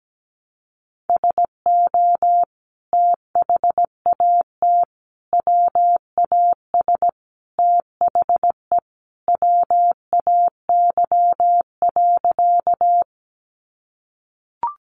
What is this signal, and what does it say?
Morse code, heard as a keyed tone, about 17 wpm, 700 Hz. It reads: SO THAT WAS THE WAY.
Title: Sentences from Top 100 Words 17wpm